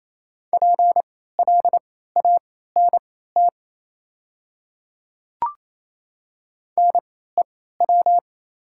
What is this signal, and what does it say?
Morse code, heard as a keyed tone, about 28 wpm, spaced as for 16 wpm, 700 Hz. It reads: PLANT NEW
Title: 3x Character Spacing - Top 300 Words - 28wpm